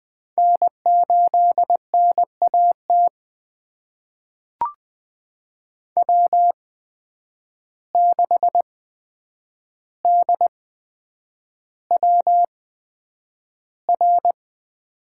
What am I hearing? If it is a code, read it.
Morse code, heard as a keyed tone, about 20 wpm, spaced as for 9 wpm, 700 Hz. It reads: N8NAT W6DWR
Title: USA Call Signs - 5 Characters - 8x Character Spacing - 20wpm